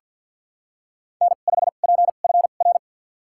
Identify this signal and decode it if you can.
Morse code, heard as a keyed tone, about 51 wpm, spaced as for 36 wpm, 700 Hz. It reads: N5PFR
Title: USA Call Signs - 5 Characters - 2x Character Spacing - 50wpm